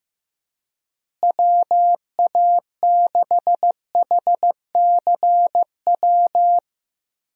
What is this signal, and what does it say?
Morse code, heard as a keyed tone, about 15 wpm, 700 Hz. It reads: WA6HCW
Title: USA Call Signs - 6 Characters - Repeated 2x - 15wpm